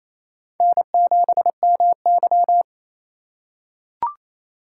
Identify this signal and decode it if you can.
Morse code, heard as a keyed tone, about 28 wpm, 700 Hz. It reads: N7MY
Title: USA Call Signs - 4x Character Spacing - 28wpm